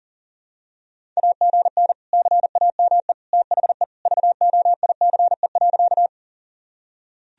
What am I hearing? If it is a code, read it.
Morse code, heard as a keyed tone, about 40 wpm, 700 Hz. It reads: AGN CAME THE VOICE.
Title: Sentences from Top 500 Words and Abbreviations 40wpm